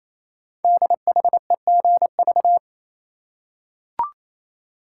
Morse code, heard as a keyed tone, about 29 wpm, 700 Hz. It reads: DHEGV
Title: Code Groups - 5 Letters - 28wpm